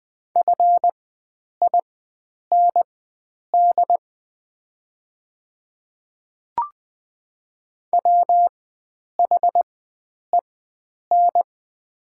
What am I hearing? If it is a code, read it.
Morse code, heard as a keyed tone, about 20 wpm, spaced as for 9 wpm, 700 Hz. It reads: FIND WHEN
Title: 4x Character Spacing - Top 100 Words - 20wpm